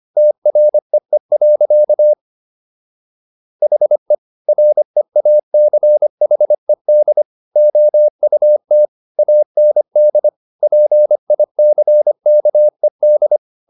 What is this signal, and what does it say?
Morse code, heard as a keyed tone, about 25 wpm, 600 Hz. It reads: TREE. HE REACHED OUT AND PICKED